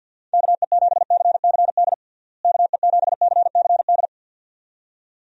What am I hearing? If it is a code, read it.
Morse code, heard as a keyed tone, about 51 wpm, 700 Hz. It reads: KE7CCD KE7CCD
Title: USA Call Signs - 6 Characters - Repeated 2x - 50wpm